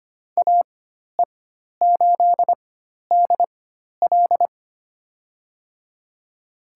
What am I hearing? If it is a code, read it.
Morse code, heard as a keyed tone, about 25 wpm, spaced as for 12 wpm, 700 Hz. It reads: AE8DL
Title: USA Call Signs - 4x Character Spacing - 25wpm